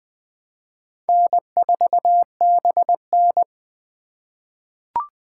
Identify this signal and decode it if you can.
Morse code, heard as a keyed tone, about 20 wpm, 700 Hz. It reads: N4BN
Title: USA Call Signs - 4 Characters - 3x Character Spacing - 20wpm